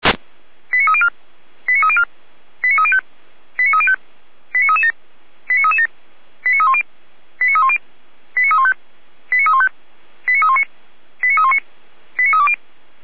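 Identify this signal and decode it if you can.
Five-tone selective call (ZVEI), heard as quick runs of five tones, tones of about 70 ms. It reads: 89304 89304 89305 89305 89337 89337 89310 89310 89315 89315 89319 89319 89320
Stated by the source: ZVEI five-tone alert received by SDR, feuerwehr-orlamuende.org fire brigade audio server.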